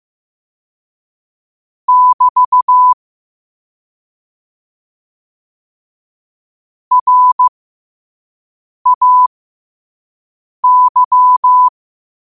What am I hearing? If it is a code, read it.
Morse code, heard as a keyed tone, about 15 wpm, spaced as for 5 wpm, 1000 Hz. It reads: <BT> RAY